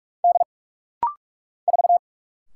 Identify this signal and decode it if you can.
Morse code, heard as a keyed tone, about 44 wpm, 700 Hz. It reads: D 4
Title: Single Letter-Number - Rapid-Fire 45wpm